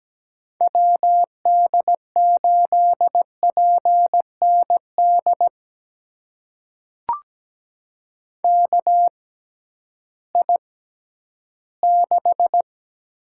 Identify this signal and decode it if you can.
Morse code, heard as a keyed tone, about 17 wpm, spaced as for 11 wpm, 700 Hz. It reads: WD8PND KI6
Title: USA Call Signs - 6 Characters - 6x Character Spacing - 17wpm